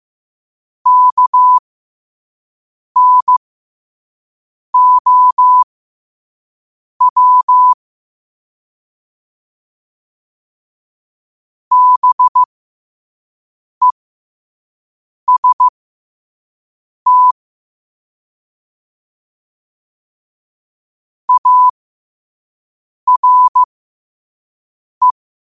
Morse code, heard as a keyed tone, about 15 wpm, spaced as for 5 wpm, 1000 Hz. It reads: KNOW BEST ARE
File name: qotdfiles / 5wpm